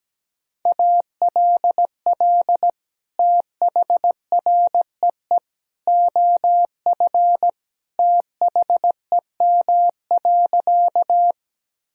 Morse code, heard as a keyed tone, about 17 wpm, 700 Hz. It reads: ALL THREE OF THEM.
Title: Sentences from Top 1000 Words - Review Part 1 - 17wpm